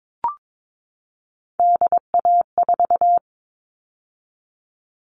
Morse code, heard as a keyed tone, about 22 wpm, 700 Hz. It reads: DA4
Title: All Cut Numbers - 3-Digit 22wpm